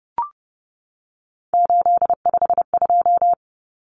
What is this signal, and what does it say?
Morse code, heard as a keyed tone, about 30 wpm, 700 Hz. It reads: 852
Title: Common Cut Numbers - 3-Digit 30wpm